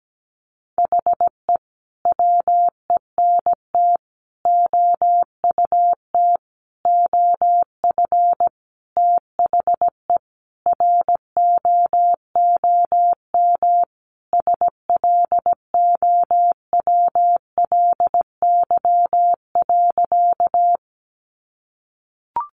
Morse code, heard as a keyed tone, about 17 wpm, 700 Hz. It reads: HE WENT OUT OF THE ROOM SLOWLY.
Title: Sentences from Top 1100 Words 17wpm